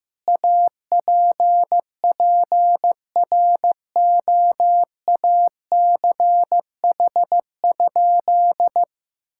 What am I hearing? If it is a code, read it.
Morse code, heard as a keyed tone, about 15 wpm, 700 Hz. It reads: APPROACH?
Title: Sentences from Top 1000 Words - Review Part 6 - 15wpm